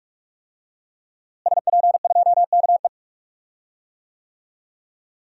Morse code, heard as a keyed tone, about 46 wpm, 700 Hz. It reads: SP2KE
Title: DXCC-100 Call Signs - 5 Characters - Repeated 3x - 45wpm